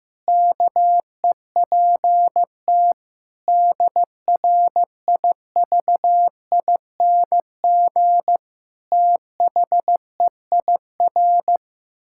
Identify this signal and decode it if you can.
Morse code, heard as a keyed tone, about 15 wpm, 700 Hz. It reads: KEPT DRIVING THEIR
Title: Pangram Challenge v2 - 15wpm